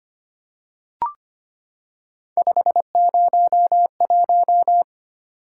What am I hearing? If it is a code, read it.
Morse code, heard as a keyed tone, about 25 wpm, 700 Hz. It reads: 501
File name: Three Letter-Number Combinations - 25wpm